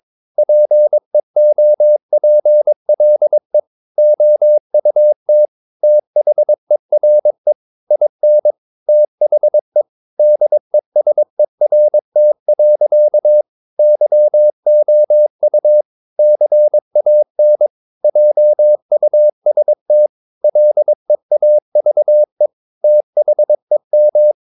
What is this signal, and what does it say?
Morse code, heard as a keyed tone, about 22 wpm, 600 Hz. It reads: PEOPLE OUT THERE IN THE DESERT. YOU CAN JUST LEAVE THEM
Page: Romanen Never af Ken Follet på engelsk.